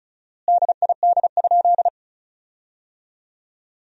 Morse code, heard as a keyed tone, about 36 wpm, 700 Hz. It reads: DID?
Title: 4x Word-Spacing - Sentences from Top 100 Words - 35wpm